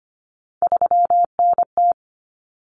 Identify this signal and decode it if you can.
Morse code, heard as a keyed tone, about 25 wpm, 700 Hz. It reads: 3NT
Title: Common Cut Numbers - 3-Digit 25wpm